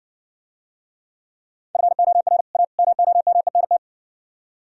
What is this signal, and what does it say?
Morse code, heard as a keyed tone, about 65 wpm, 700 Hz. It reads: FOR A MOMENT